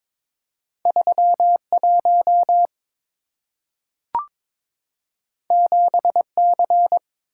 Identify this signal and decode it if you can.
Morse code, heard as a keyed tone, about 22 wpm, 690 Hz. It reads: 31 7C
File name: practice / 22wpm / Two Letter-Number Combinations - 22wpm